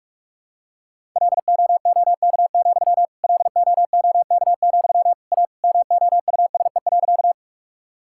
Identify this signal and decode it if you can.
Morse code, heard as a keyed tone, about 44 wpm, 700 Hz. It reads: LOOK, LOOK, A MOUSE.
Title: Sentences from Top 1000 Words - Review Part 5 - 45wpm